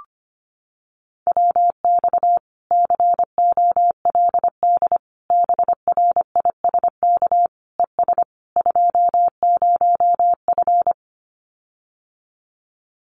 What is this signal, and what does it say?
Morse code, heard as a keyed tone, about 25 wpm, 700 Hz. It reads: WX COLD BRISK ES 20F
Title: Common QSO Phrases Encore 25wpm